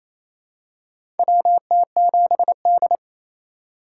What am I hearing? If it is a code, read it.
Morse code, heard as a keyed tone, about 28 wpm, 700 Hz. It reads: WT7D
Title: USA Call Signs - 4 Characters - 8x Character Spacing - 28wpm